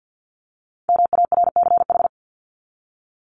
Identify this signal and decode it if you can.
Morse code, heard as a keyed tone, about 48 wpm, 700 Hz. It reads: MARCH
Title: Top 1501-1750 Words 50wpm